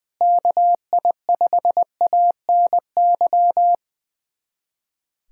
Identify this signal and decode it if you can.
Morse code, heard as a keyed tone, about 20 wpm, 700 Hz. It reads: KI5ANY
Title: USA Callsign DE Callsign - 6 Characters - 20wpm